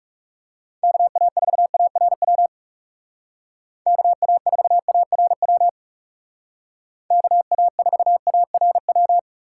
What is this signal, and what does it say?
Morse code, heard as a keyed tone, about 37 wpm, 700 Hz. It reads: KA4ARW KA4ARW KA4ARW
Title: Speed Racing - USA 2x3 Call Signs 30wpm